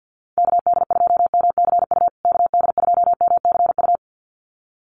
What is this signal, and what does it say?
Morse code, heard as a keyed tone, about 50 wpm, 700 Hz. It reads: KD2MCU KD2MCU
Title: USA Call Signs Repeated 2x 50wpm